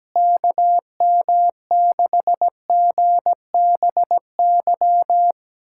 Morse code, heard as a keyed tone, about 17 wpm, 700 Hz. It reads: KM6GBY
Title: USA Callsign DE Callsign - 6 Characters - 17wpm